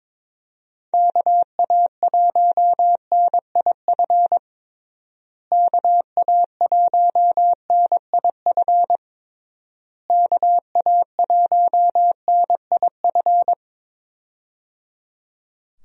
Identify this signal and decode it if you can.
Morse code, heard as a keyed tone, about 22 wpm, 700 Hz. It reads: KA1NIF KA1NIF KA1NIF
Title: USA Call Signs - 6 Characters - Repeated 3x - 22wpm